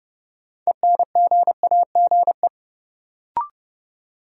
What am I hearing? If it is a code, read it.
Morse code, heard as a keyed tone, about 30 wpm, 700 Hz. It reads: ENGAGE
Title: Sentences from Top 1500 Words - New Word Drill Alt - 30wpm